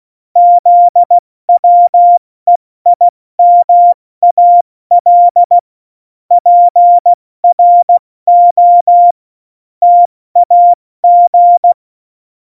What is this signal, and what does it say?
Morse code, heard as a keyed tone, about 16 wpm, 700 Hz. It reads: ZWEIMAL PRO TAG
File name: Morsekurs / Witze / Witz 06. Oktober 2020, 70 F80